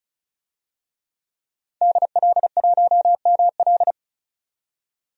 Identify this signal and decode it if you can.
Morse code, heard as a keyed tone, about 36 wpm, 700 Hz. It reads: DL1ML
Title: DXCC-100 Call Signs - 5 Characters - 35wpm